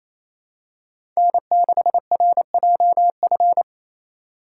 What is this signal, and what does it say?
Morse code, heard as a keyed tone, about 28 wpm, 700 Hz. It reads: N6RJF
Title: USA Call Signs - 5 Characters - 4x Character Spacing - 28wpm